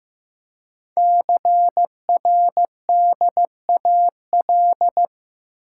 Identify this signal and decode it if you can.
Morse code, heard as a keyed tone, about 15 wpm, 700 Hz. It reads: CRDAL
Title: Code Groups - 5 Letters - 15wpm